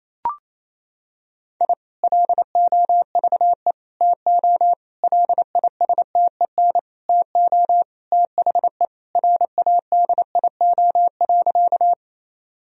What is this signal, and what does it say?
Morse code, heard as a keyed tone, about 28 wpm, 700 Hz. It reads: I LOVE TO LISTEN TO THE RADIO.
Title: Sentences from Top 1000 Words - Review Part 3 - 28wpm